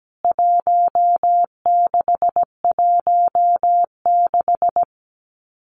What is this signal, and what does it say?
Morse code, heard as a keyed tone, about 17 wpm, 700 Hz. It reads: 1616
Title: CWT Exchange 17wpm